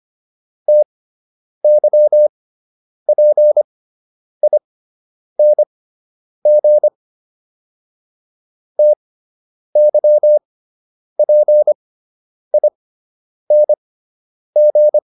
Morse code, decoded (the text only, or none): TYPING TYPING